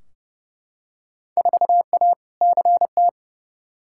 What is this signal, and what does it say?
Morse code, heard as a keyed tone, about 31 wpm, 700 Hz. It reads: 4A CT
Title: Field Day Exchanges Encore - 30wpm